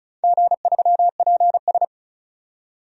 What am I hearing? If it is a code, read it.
Morse code, heard as a keyed tone, about 36 wpm, 700 Hz. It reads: G3PS
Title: DXCC-100 Call Signs - 4 Characters - 35wpm